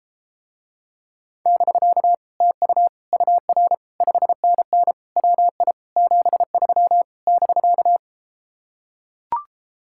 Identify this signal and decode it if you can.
Morse code, heard as a keyed tone, about 33 wpm, 700 Hz. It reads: <BK> TU UR 5NN WI 73 <BK>